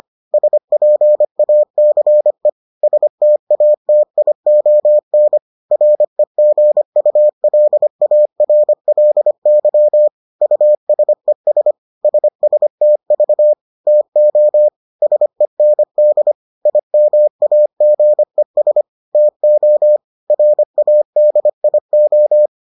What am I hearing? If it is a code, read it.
Morse code, heard as a keyed tone, about 25 wpm, 600 Hz. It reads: SPACE STATION REGULARLY USES SSTV TO SEND IMAGES TO RADIO